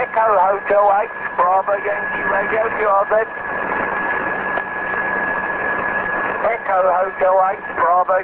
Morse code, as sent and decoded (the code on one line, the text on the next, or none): none